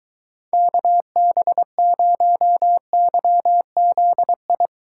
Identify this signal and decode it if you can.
Morse code, heard as a keyed tone, about 23 wpm, 700 Hz. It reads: KB0YZI